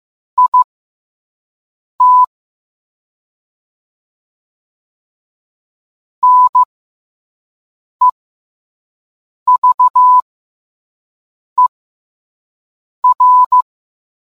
Morse code, heard as a keyed tone, about 15 wpm, spaced as for 5 wpm, 1000 Hz. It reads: IT NEVER